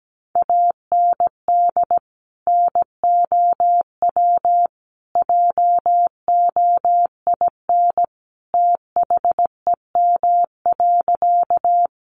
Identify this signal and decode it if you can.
Morse code, heard as a keyed tone, about 17 wpm, 700 Hz. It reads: AND NOW JOIN THEM.
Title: Sentences from Top 1400 Words 17wpm